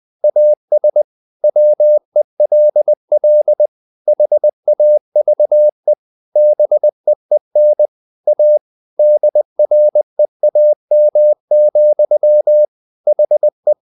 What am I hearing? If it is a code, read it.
Morse code, heard as a keyed tone, about 20 wpm, 600 Hz. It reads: AS WELL HAVE BEEN A DREAM, HE
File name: the old man and the sea part 2 20wpm